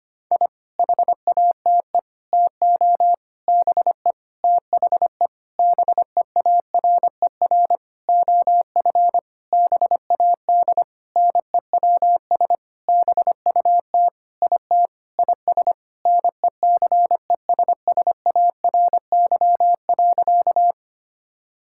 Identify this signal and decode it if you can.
Morse code, heard as a keyed tone, about 25 wpm, 700 Hz. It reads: I HATE TO BE THE BEARER OF BAD NEWS BUT IT IS NECESSARY.
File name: practice / 25wpm / Sentences with 9-Word Phrases - 25wpm